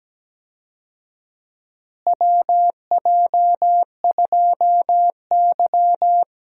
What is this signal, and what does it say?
Morse code, heard as a keyed tone, about 17 wpm, 700 Hz. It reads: WJ2Y
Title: USA Call Signs - 4 Characters - Repeated 2x - 17wpm